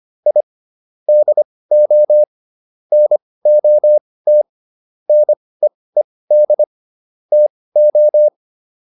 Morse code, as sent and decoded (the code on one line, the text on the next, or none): .. / -.. --- / -. --- - / -. . -.. / - ---
I DO NOT NEED TO